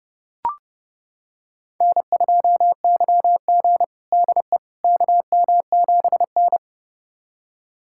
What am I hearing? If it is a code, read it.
Morse code, heard as a keyed tone, about 30 wpm, 700 Hz. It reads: N2YG DE KM7N
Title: USA Callsign DE Callsign - 4 Characters - 30wpm